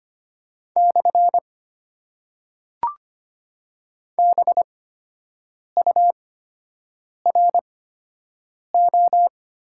Morse code, heard as a keyed tone, about 25 wpm, spaced as for 7 wpm, 690 Hz. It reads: / BURO
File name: Top 90 QSO Elements - 8x Character Spacing - 25wpm